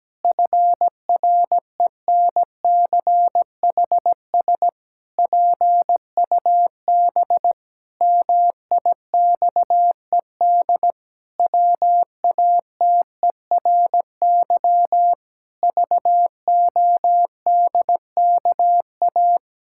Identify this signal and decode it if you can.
Morse code, heard as a keyed tone, about 17 wpm, 700 Hz. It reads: FRENCHS PUB MIXED WATERY VODKA
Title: Pangram Challenge v2 - 17wpm